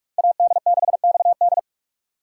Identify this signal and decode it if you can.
Morse code, heard as a keyed tone, about 42 wpm, 700 Hz. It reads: AD6XD